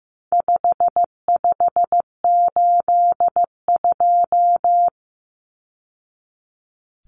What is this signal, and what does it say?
Morse code, heard as a keyed tone, about 15 wpm, 700 Hz. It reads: 5582
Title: Numbers - 4-Digit 15wpm